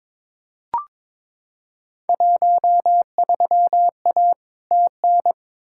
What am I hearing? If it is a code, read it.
Morse code, heard as a keyed tone, about 21 wpm, 700 Hz. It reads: 13A TN
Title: Field Day Exchanges Encore - 22wpm